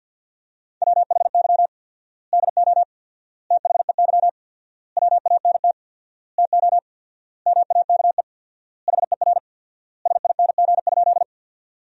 Morse code, heard as a keyed tone, about 50 wpm, 700 Hz. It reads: WHY DO THEY WANT TO MAKE HER SING?
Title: 4x Word-Spacing - Sentences from Top 400 Words - 50wpm